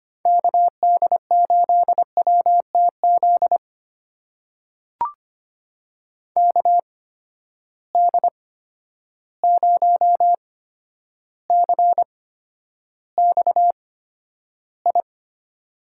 Morse code, heard as a keyed tone, about 25 wpm, spaced as for 11 wpm, 700 Hz. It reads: KD8WTZ KD0CXI
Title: USA Call Signs - 6 Characters - 8x Character Spacing - 25wpm